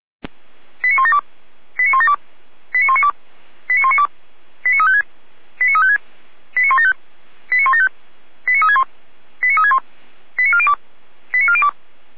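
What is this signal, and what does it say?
Five-tone selective call (ZVEI), heard as quick runs of five tones, tones of about 70 ms. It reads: 89172 89172 89192 89192 89356 89356 89175 89175 89361 89361 89402 89402
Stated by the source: ZVEI five-tone selective call received on an SDR, recorded by Feuerwehr Orlamünde fire brigade.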